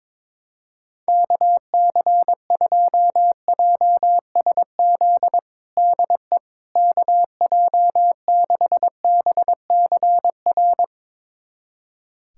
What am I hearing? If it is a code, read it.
Morse code, heard as a keyed tone, about 21 wpm, 700 Hz. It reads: KC2JSZ DE KJ6BCR
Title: USA Callsign DE Callsign - 6 Characters - 22wpm